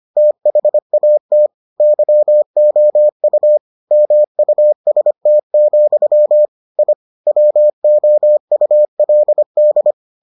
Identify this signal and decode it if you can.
Morse code, heard as a keyed tone, about 25 wpm, 600 Hz. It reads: THAT YOU MUST, I WOULD